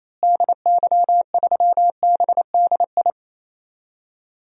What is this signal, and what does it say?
Morse code, heard as a keyed tone, about 28 wpm, 700 Hz. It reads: DY3BDI
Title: DXCC-100 Call Signs - 6 Characters - Repeated 3x - 28wpm